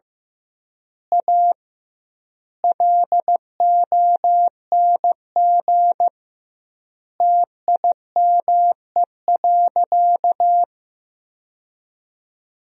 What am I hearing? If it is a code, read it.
Morse code, heard as a keyed tone, about 15 wpm, 700 Hz. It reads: A LONG TIME.
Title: 2x Word-Spacing - Sentences from Top 400 Words - 15wpm